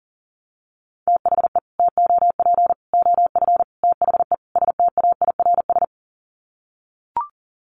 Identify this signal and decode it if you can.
Morse code, heard as a keyed tone, about 40 wpm, 700 Hz. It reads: THE TOP OF THE STAIRS
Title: Sets of 5 Words Encore 40wpm